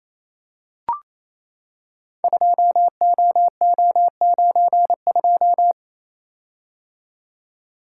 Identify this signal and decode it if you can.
Morse code, heard as a keyed tone, about 29 wpm, 700 Hz. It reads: 2OO92
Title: Code Groups - 5 Letter:Numbers - 28wpm